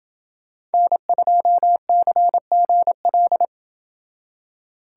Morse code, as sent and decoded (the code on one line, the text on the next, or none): -. ..--- -.-. --. .-..
N2CGL